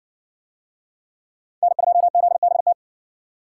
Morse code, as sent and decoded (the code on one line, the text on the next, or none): -. ..--- --.. -... -
N2ZBT